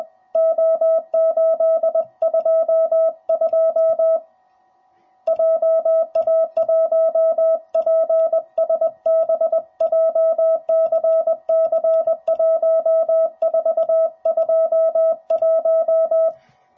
Morse code, heard as a keyed tone, about 20 wpm, 650 Hz. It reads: O822 JA1PSB JCC1421
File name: jh8cbh / 2021cw / cw05b